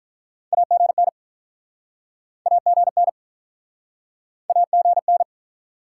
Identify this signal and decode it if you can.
Morse code, heard as a keyed tone, about 50 wpm, 700 Hz. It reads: AGN AGN AGN